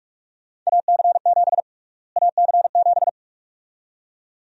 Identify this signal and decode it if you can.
Morse code, heard as a keyed tone, about 46 wpm, 700 Hz. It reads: AC7 AC7
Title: USA Call Sign Prefixes - Repeated 2x - 45wpm